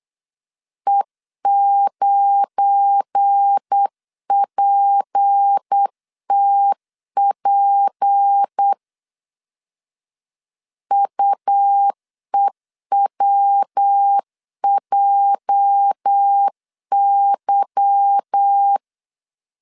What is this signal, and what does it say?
Morse code, heard as a keyed tone, about 8 wpm, 800 Hz. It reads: E9PTP UEWJY
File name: LEZIONI / 8 YJ45 / 8 YJ 8wpm 3313